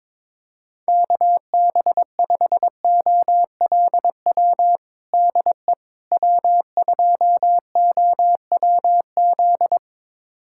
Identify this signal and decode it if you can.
Morse code, heard as a keyed tone, about 22 wpm, 700 Hz. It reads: KB5OLW DE W2OWZ
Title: USA Callsign DE Callsign 22wpm